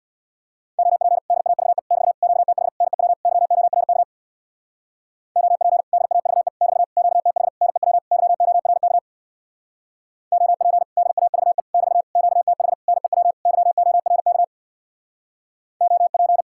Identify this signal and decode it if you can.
Morse code, heard as a keyed tone, about 59 wpm, 700 Hz. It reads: OP DAVE <BT> QTH NEW YORK OP DAVE <BT> QTH NEW YORK OP DAVE <BT> QTH NEW YORK OP
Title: Speed-Racing Common QSO Phrases - 50wpm